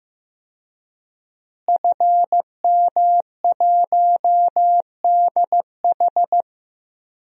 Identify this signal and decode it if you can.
Morse code, heard as a keyed tone, about 15 wpm, 700 Hz. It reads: FM1DH
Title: DXCC-100 Call Signs - 5 Characters - Repeated 2x - 15wpm